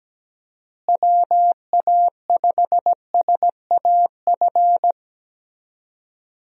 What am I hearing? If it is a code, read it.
Morse code, heard as a keyed tone, about 17 wpm, 700 Hz. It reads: WA5SAF
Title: USA Call Signs - 6 Characters - Repeated 2x - 17wpm